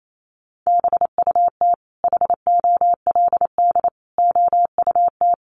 Morse code, heard as a keyed tone, about 28 wpm, 700 Hz. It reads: BUT HOLD OUT
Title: Sentences from Top 1300 Words 28wpm